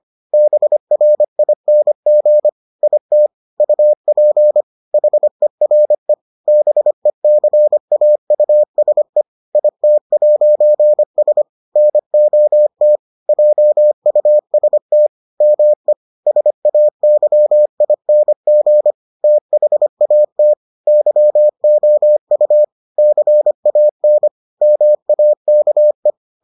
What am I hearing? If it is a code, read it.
Morse code, heard as a keyed tone, about 25 wpm, 600 Hz. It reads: BRING IT UP HERE BECAUSE IT'S NOT JUST ME SAYING THAT YOU CAN MAKE